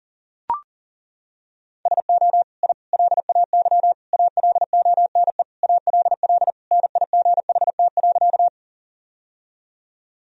Morse code, heard as a keyed tone, about 40 wpm, 700 Hz. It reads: SO I LAY ALONE ALL NIGHT.
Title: Sentences from Top 1000 Words - Review Part 2 - 40wpm